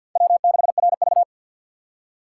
Morse code, heard as a keyed tone, about 47 wpm, 700 Hz. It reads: W6RV
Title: USA Call Signs - 4 Characters - 6x Character Spacing - 50wpm